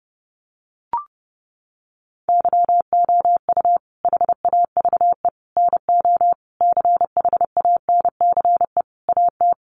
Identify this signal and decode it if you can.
Morse code, heard as a keyed tone, about 30 wpm, 700 Hz. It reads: YOU HAVE NO CHANCE AT
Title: Sentences from Top 800 Words 30wpm